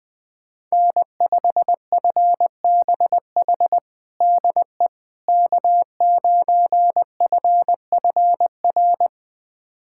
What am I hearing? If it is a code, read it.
Morse code, heard as a keyed tone, about 20 wpm, 700 Hz. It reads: N5FBH DE K9FFR
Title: USA Callsign DE Callsign - 5 Characters - 20wpm